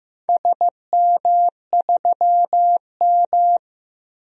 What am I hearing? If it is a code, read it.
Morse code, heard as a keyed tone, about 15 wpm, 700 Hz. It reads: SM3M
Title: DXCC-100 Call Signs - 4 Characters - Repeated 2x - 15wpm